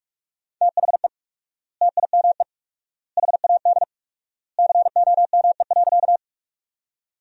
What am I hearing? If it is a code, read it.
Morse code, heard as a keyed tone, about 44 wpm, 700 Hz. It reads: THE TIME HAD COME.
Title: 4x Word-Spacing - Sentences from Top 100 Words - 45wpm